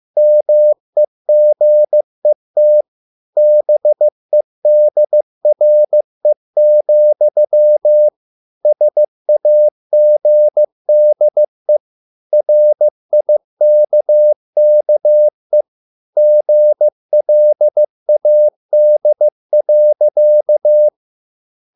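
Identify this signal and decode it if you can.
Morse code, heard as a keyed tone, about 15 wpm, 600 Hz. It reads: MEGET BEDRE, SAGDE RIKKE GLAD.
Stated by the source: Morse code at a slow 15 wpm from CW med Gnister.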